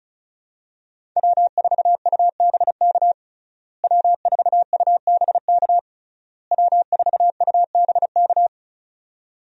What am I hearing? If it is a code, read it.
Morse code, heard as a keyed tone, about 36 wpm, 700 Hz. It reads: W4UBK W4UBK W4UBK
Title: USA Call Signs - 5 Characters - Repeated 3x - 35wpm